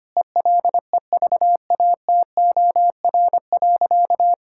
Morse code, heard as a keyed tone, about 25 wpm, 700 Hz. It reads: ELEVATOR.
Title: Sentences from Top 1600 Words 25wpm